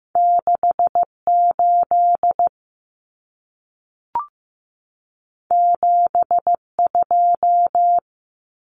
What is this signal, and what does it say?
Morse code, heard as a keyed tone, about 15 wpm, 700 Hz. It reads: 6 8 7 2
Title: Common Cut Numbers - 2-Digit 15wpm